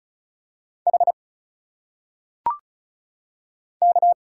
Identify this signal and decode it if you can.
Morse code, heard as a keyed tone, about 36 wpm, 700 Hz. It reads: H K
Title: Single Letters - Phonetic Pronunciation 35wpm